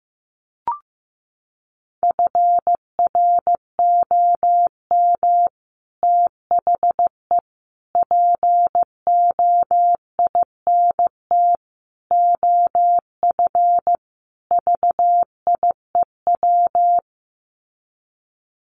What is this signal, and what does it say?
Morse code, heard as a keyed tone, about 15 wpm, 700 Hz. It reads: FROM THE POINT OF VIEW
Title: Sets of 5 Words 15wpm